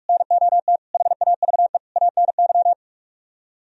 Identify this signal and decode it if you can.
Morse code, heard as a keyed tone, about 44 wpm, 700 Hz. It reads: NOT HAVE ANY